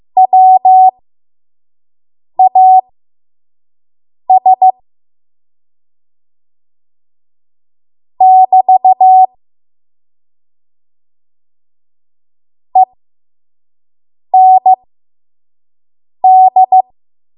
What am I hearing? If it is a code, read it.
Morse code, heard as a keyed tone, about 15 wpm, spaced as for 5 wpm, 750 Hz. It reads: WAS <BT> END
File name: Morse / 05WPM